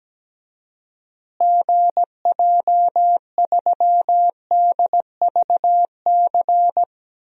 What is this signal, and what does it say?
Morse code, heard as a keyed tone, about 17 wpm, 700 Hz. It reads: GJ3DVC
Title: DXCC-100 Call Signs - 6 Characters - Repeated 2x - 17wpm